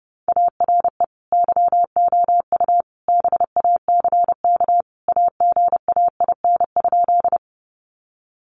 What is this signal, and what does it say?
Morse code, heard as a keyed tone, about 30 wpm, 700 Hz. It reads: ARE YOU BACK AGAIN?
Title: Sentences from Top 200 Words 30wpm